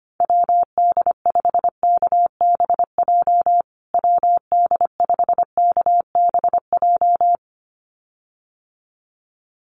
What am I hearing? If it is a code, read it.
Morse code, heard as a keyed tone, about 25 wpm, 700 Hz. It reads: WD5KBJ WD5KBJ
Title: USA Call Signs Repeated 2x 25wpm